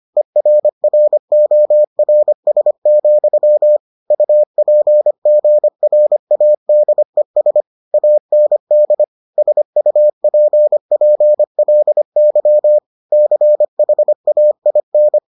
Morse code, heard as a keyed tone, about 25 wpm, 600 Hz. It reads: ERRORS, UPGRADES AND SUPPLY CHAIN